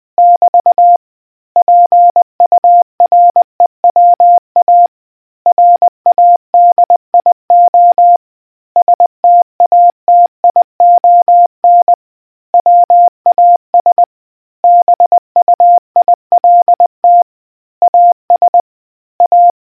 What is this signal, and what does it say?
Morse code, heard as a keyed tone, about 20 wpm, 700 Hz. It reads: <BT> PUREWA RADIO STATION WAS BUILT AS A